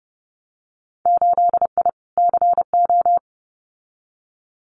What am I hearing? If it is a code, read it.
Morse code, heard as a keyed tone, about 30 wpm, 700 Hz. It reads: 8I CO
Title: Winter Field Day Exchanges Encore 30wpm